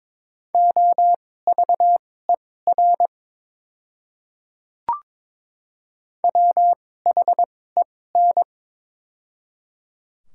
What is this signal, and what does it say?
Morse code, heard as a keyed tone, about 22 wpm, spaced as for 16 wpm, 700 Hz. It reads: OVER WHEN
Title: 2x Character Spacing - Top 100 Words - 22wpm